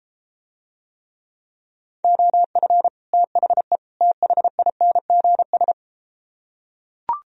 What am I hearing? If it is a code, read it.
Morse code, heard as a keyed tone, about 33 wpm, 700 Hz. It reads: OF THE THINGS